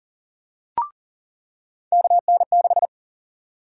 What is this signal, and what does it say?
Morse code, heard as a keyed tone, about 40 wpm, 700 Hz. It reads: KN6
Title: USA Call Sign Prefixes - 40wpm